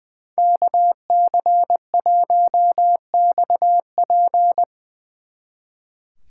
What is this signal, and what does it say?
Morse code, heard as a keyed tone, about 22 wpm, 700 Hz. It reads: KC1XP
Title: Speed Racing - USA Call Signs 20wpm